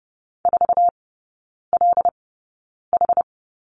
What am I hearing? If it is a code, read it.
Morse code, heard as a keyed tone, about 30 wpm, spaced as for 9 wpm, 690 Hz. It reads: 4LH
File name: Lesson 17e - Letter u - Three Characters 30wpm